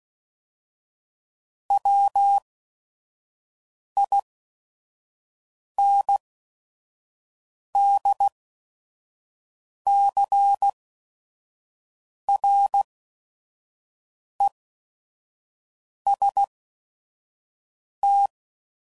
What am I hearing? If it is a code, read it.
Morse code, heard as a keyed tone, about 16 wpm, spaced as for 5 wpm, 800 Hz. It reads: WINDCREST
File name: CW Code Groups @5WPM / fromSM's / Cities